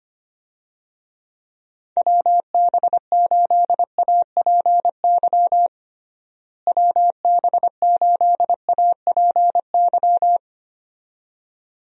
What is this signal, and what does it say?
Morse code, heard as a keyed tone, about 25 wpm, 700 Hz. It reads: WB8APY WB8APY
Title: USA Call Signs - 6 Characters - Repeated 2x - 25wpm